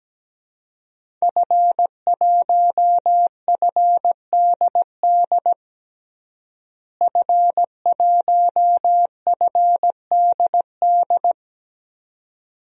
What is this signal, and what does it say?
Morse code, heard as a keyed tone, about 17 wpm, 700 Hz. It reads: F1FDD F1FDD
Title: DXCC-100 Call Signs - 5 Characters - Repeated 2x - 17wpm